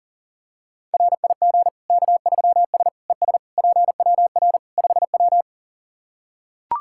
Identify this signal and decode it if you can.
Morse code, heard as a keyed tone, about 40 wpm, 700 Hz. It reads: RIG K3S ES PWR 5W
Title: K1USN SST QSO Practice 40wpm